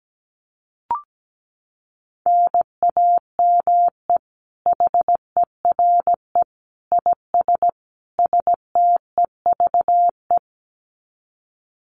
Morse code, heard as a keyed tone, about 17 wpm, 700 Hz. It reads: NAME HERE IS STEVE
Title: Common QSO Phrases 17wpm